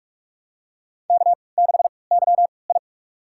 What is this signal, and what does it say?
Morse code, heard as a keyed tone, about 46 wpm, spaced as for 25 wpm, 700 Hz. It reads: K6YI
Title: USA Call Signs - 4 Characters - 3x Character Spacing - 45wpm